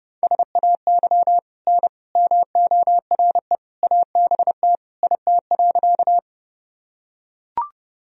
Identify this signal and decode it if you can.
Morse code, heard as a keyed tone, about 31 wpm, 700 Hz. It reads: SAY N MORE ABT IT.
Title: Sentences from Top 200 Words and Abbreviations 30wpm